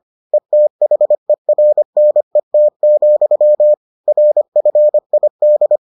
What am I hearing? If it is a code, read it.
Morse code, heard as a keyed tone, about 25 wpm, 600 Hz. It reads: ETHERNET, RFID